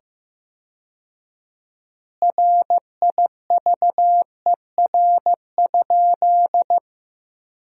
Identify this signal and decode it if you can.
Morse code, heard as a keyed tone, about 15 wpm, 700 Hz. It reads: RIVER?
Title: 4x Word-Spacing - Sentences from Top 300 Words - 15wpm